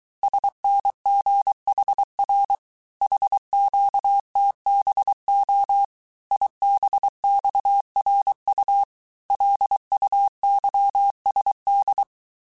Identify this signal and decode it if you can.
Morse code, heard as a keyed tone, about 23 wpm, 800 Hz. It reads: SNGHR HQTBO IBXRU LUYSD
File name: cours-du-mercredi / audios / cours-855-texte-chiffre-vitesse-22-mpm